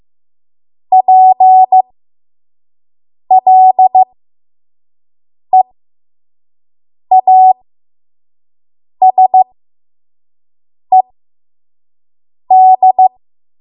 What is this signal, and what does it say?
Morse code, heard as a keyed tone, about 15 wpm, spaced as for 5 wpm, 750 Hz. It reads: PLEASED
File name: Morse / 05WPM